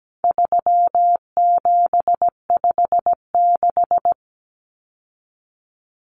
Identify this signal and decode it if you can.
Morse code, heard as a keyed tone, about 17 wpm, 700 Hz. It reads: 3756
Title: Common Cut Numbers - 4-Digit 17wpm